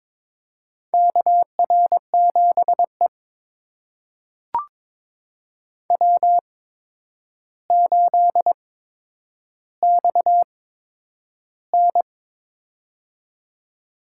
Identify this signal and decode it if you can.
Morse code, heard as a keyed tone, about 22 wpm, spaced as for 9 wpm, 700 Hz. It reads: KR7E W8XN
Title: USA Call Signs - 4 Characters - 8x Character Spacing - 22wpm